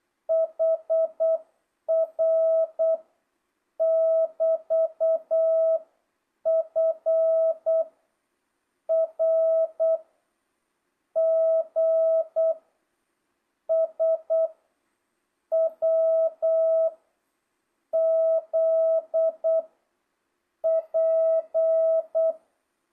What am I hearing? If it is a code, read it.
Morse code, heard as a keyed tone, about 8 wpm, 660 Hz. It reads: HR <BT> F R G S W Z P
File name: jh8cbh / 2025cw